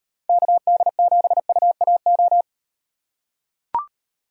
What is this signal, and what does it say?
Morse code, heard as a keyed tone, about 38 wpm, 700 Hz. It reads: KD7UAO